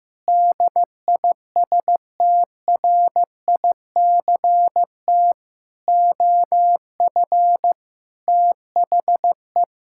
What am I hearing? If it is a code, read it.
Morse code, heard as a keyed tone, about 15 wpm, 700 Hz. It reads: DISTRICT OF THE
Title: Sentences from Top 1600 Words 15wpm